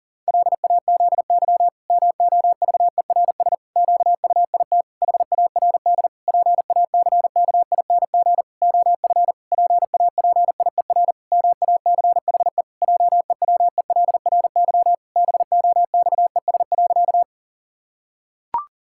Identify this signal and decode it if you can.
Morse code, heard as a keyed tone, about 40 wpm, 700 Hz. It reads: LAZY MOVERS QUIT HARD PACKING OF PAPIER MACHE JEWELRY BOXES.
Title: Pangram Challenge v2 - 40wpm